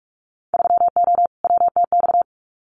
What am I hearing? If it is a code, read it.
Morse code, heard as a keyed tone, about 46 wpm, 690 Hz. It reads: 3O WTX
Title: Winter Field Day Exchanges Encore 45wpm